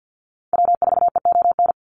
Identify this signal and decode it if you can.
Morse code, heard as a keyed tone, about 50 wpm, 700 Hz. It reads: W4EON
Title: NA SOTA Activator Callsigns 50wpm